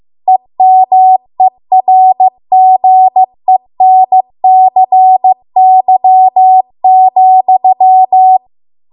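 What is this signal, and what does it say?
Morse code, heard as a keyed tone, about 15 wpm, 750 Hz. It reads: EMERGENCY,
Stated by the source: Morse code practice at 15 wpm from the national amateur radio association's code archive.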